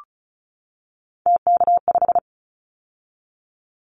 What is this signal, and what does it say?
Morse code, heard as a keyed tone, about 34 wpm, 700 Hz. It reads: TK5
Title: Callsign Prefixes for Top 100 DXCC - 35wpm